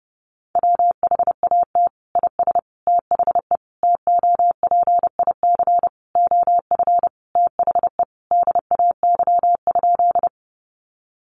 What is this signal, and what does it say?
Morse code, heard as a keyed tone, about 30 wpm, 700 Hz. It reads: WHAT IS THE TOPIC OF THE DAY?
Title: Sentences from Top 1200 Words 30wpm